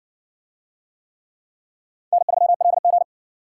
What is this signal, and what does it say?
Morse code, heard as a keyed tone, about 60 wpm, 700 Hz. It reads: N3LG